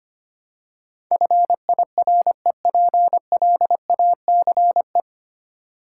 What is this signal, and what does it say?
Morse code, heard as a keyed tone, about 25 wpm, 700 Hz. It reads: FIREPLACE
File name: Easy Words - Speed-Racing with Hints - 25wpm